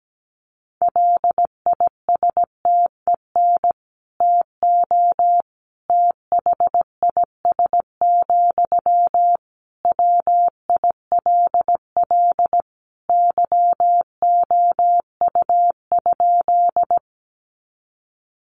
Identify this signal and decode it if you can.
Morse code, heard as a keyed tone, about 17 wpm, 700 Hz. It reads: LISTEN TO THIS, WILL YOU?
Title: Sentences from Top 400 Words 17wpm